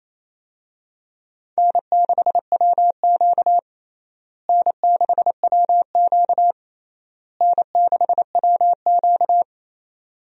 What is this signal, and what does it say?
Morse code, heard as a keyed tone, about 28 wpm, 700 Hz. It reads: N6WQ N6WQ N6WQ
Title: USA Call Signs - 4 Characters - Repeated 3x - 28wpm